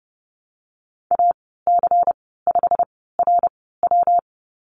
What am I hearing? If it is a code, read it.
Morse code, heard as a keyed tone, about 30 wpm, spaced as for 17 wpm, 700 Hz. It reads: AC5RW
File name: Lesson 19g - Letter w - Call Signs 30wpm